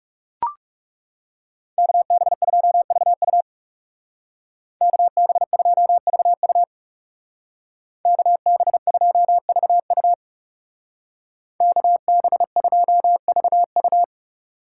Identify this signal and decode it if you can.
Morse code, heard as a keyed tone, about 35 wpm, 700 Hz. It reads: KB2VU KB2VU KB2VU KB2VU
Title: Speed Racing - USA 2x2 Call Signs 30wpm